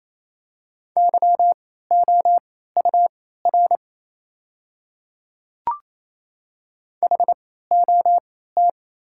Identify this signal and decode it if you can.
Morse code, heard as a keyed tone, about 28 wpm, spaced as for 16 wpm, 700 Hz. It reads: YOUR HOT
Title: 3x Character Spacing - Top 100 Words - 28wpm